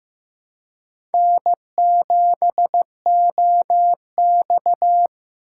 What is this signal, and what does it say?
Morse code, heard as a keyed tone, about 15 wpm, 700 Hz. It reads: N7OX
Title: USA Call Signs - 4 Characters - 4x Character Spacing - 15wpm